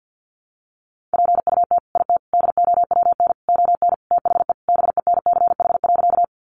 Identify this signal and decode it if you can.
Morse code, heard as a keyed tone, about 50 wpm, 700 Hz. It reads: PUT IT DOWN ON THE BENCH.
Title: Sentences from Top 1400 Words 50wpm